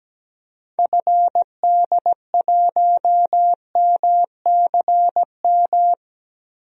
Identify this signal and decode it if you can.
Morse code, heard as a keyed tone, about 17 wpm, 700 Hz. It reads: FD1MCM
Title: DXCC-100 Call Signs - 6 Characters - Repeated 3x - 17wpm